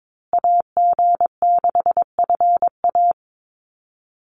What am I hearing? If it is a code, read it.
Morse code, heard as a keyed tone, about 22 wpm, 690 Hz. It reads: AG6FA
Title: USA Call Signs Repeated 3x 22wpm